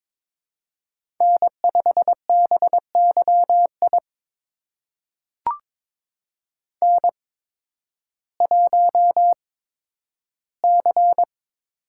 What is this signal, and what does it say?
Morse code, heard as a keyed tone, about 22 wpm, spaced as for 12 wpm, 700 Hz. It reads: N5BYI N1C
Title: USA Call Signs - 5 Characters - 8x Character Spacing - 22wpm